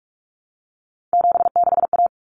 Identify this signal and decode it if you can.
Morse code, heard as a keyed tone, about 44 wpm, 700 Hz. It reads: 76A
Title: Common Cut Numbers - 3-Digit 45wpm